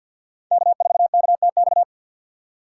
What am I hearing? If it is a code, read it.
Morse code, heard as a keyed tone, about 52 wpm, 700 Hz. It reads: K4KTX
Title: USA Call Signs - 5 Characters - 2x Character Spacing - 50wpm